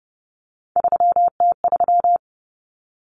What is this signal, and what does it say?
Morse code, heard as a keyed tone, about 30 wpm, 700 Hz. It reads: 3T3
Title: Common Cut Numbers - 3-Digit 30wpm